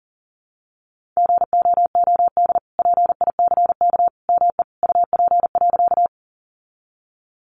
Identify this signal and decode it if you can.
Morse code, heard as a keyed tone, about 40 wpm, 700 Hz. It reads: GOOD PICK ME UP.
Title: Sentences from Top 1000 Words 40wpm